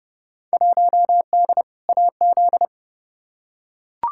Morse code, heard as a keyed tone, about 30 wpm, 700 Hz. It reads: 1D AZ
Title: Field Day Exchanges - 30wpm